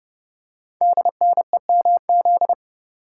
Morse code, heard as a keyed tone, about 30 wpm, 700 Hz. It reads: DNEMZ
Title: Code Groups - 5 Letters - 30wpm